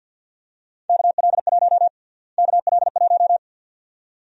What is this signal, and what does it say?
Morse code, heard as a keyed tone, about 51 wpm, 700 Hz. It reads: KL1 KL1
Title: USA Call Sign Prefixes - Repeated 2x - 50wpm